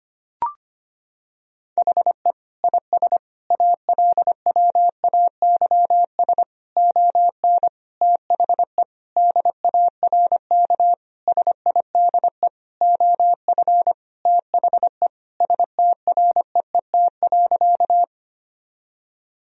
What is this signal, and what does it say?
Morse code, heard as a keyed tone, about 25 wpm, 700 Hz. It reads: HE IS ALWAYS ON THE DARK SIDE OF THE STREET.
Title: Sentences from Top 1000 Words - Review Part 3 - 25wpm